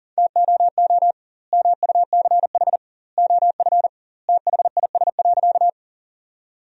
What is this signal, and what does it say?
Morse code, heard as a keyed tone, about 40 wpm, 700 Hz. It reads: TOO MUCH OF THIS.
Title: 2x Word-Spacing - Sentences from Top 200 Words - 40wpm